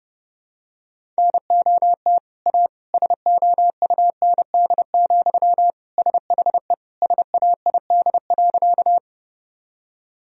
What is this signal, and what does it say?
Morse code, heard as a keyed tone, about 30 wpm, 700 Hz. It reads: NOT A SOUND, SHE SAID.
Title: Sentences from Top 1000 Words - Review Part 1 - 30wpm